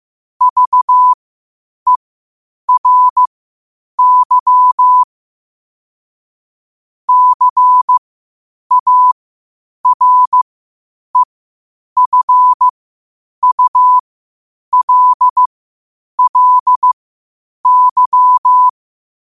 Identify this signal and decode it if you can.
Morse code, heard as a keyed tone, about 15 wpm, spaced as for 8 wpm, 1000 Hz. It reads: VERY CAREFULLY
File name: qotdfiles / 8wpm